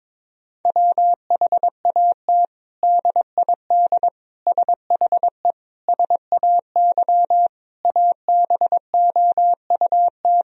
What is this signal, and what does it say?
Morse code, heard as a keyed tone, about 22 wpm, 700 Hz. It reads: WHAT DID SHE SAY ABOUT
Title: Sentences from Top 1000 Words - Review Part 2 - 22wpm